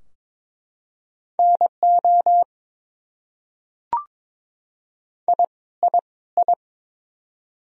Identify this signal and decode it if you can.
Morse code, heard as a keyed tone, about 22 wpm, 700 Hz. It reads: NO I I I
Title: Top 100 Words Repeated 22wpm